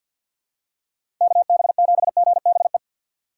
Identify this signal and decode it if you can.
Morse code, heard as a keyed tone, about 53 wpm, 700 Hz. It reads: KB7GBE
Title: USA Call Signs - 6 Characters - 4x Character Spacing - 50wpm